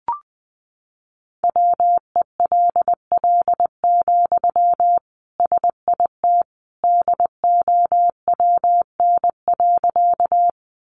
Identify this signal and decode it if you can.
Morse code, heard as a keyed tone, about 20 wpm, 700 Hz. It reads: WELL, SIT DOWN.
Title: Sentences from Top 500 Words 20wpm